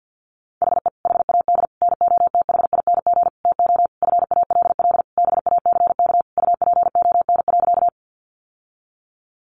Morse code, heard as a keyed tone, about 50 wpm, 700 Hz. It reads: HE HAD NOTHING TO FALL BACK UPON.
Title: Sentences from Top 1100 Words 50wpm